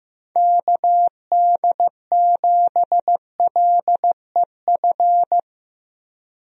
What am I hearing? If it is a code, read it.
Morse code, heard as a keyed tone, about 15 wpm, 700 Hz. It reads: KD7LEF
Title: USA Call Signs - 6 Characters - Repeated 2x - 15wpm